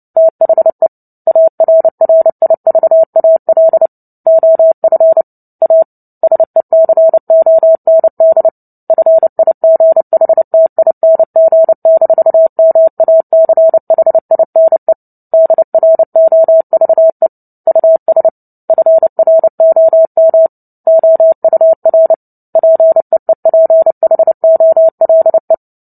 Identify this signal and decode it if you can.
Morse code, heard as a keyed tone, about 30 wpm, 650 Hz. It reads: THE ARRIVAL OF A SECOND FIGHTING-MACHINE DROVE US FROM OUR PEEPHOLE
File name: Book Two / 03-Chapter 3 - 29 WPM